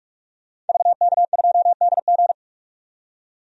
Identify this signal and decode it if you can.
Morse code, heard as a keyed tone, about 46 wpm, 700 Hz. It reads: VK2DG
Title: DXCC-100 Call Signs - 5 Characters - 45wpm